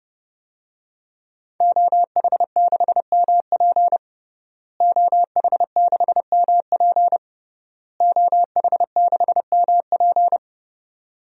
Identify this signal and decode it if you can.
Morse code, heard as a keyed tone, about 30 wpm, 700 Hz. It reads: OH6MP OH6MP OH6MP
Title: DXCC-100 Call Signs - 5 Characters - Repeated 3x - 30wpm